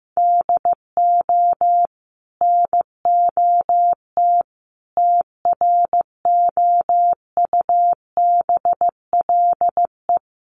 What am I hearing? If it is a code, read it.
Morse code, heard as a keyed tone, about 15 wpm, 700 Hz. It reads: DO NOT TROUBLE
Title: Sentences from Top 1200 Words 15wpm